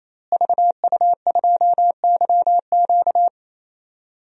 Morse code, heard as a keyed tone, about 27 wpm, 700 Hz. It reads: VU2YQ
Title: DXCC-100 Call Signs - 5 Characters - Repeated 3x - 28wpm